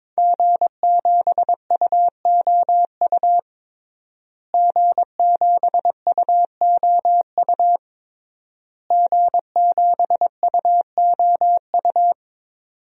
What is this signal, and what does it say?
Morse code, heard as a keyed tone, about 22 wpm, 700 Hz. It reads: G7UOU G7UOU G7UOU
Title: DXCC-100 Call Signs - 5 Characters - Repeated 3x - 22wpm